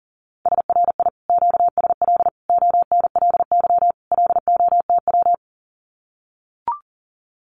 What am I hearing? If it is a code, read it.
Morse code, heard as a keyed tone, about 41 wpm, 700 Hz. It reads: SRI QSL ONLY LOTW
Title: Common QSO Phrases Encore 40wpm